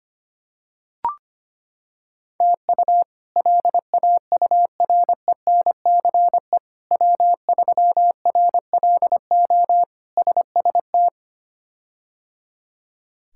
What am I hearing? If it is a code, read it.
Morse code, heard as a keyed tone, about 25 wpm, 700 Hz. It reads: TU LAURENCE W3RLO SST
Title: K1USN SST QSO Practice - 25wpm